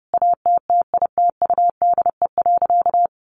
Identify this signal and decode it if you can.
Morse code, heard as a keyed tone, about 30 wpm, 700 Hz. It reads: ATTITUDE.
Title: Sentences from Top 1300 Words 30wpm